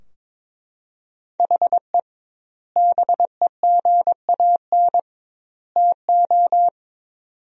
Morse code, heard as a keyed tone, about 22 wpm, 700 Hz. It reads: HE BEGAN TO
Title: 2x Word-Spacing - Sentences from Top 500 Words - 22wpm